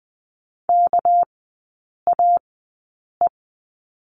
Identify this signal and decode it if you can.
Morse code, heard as a keyed tone, about 20 wpm, spaced as for 8 wpm, 690 Hz. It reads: KAE
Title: 198 Lesson 32e - Letter k - Three Characters 20wpm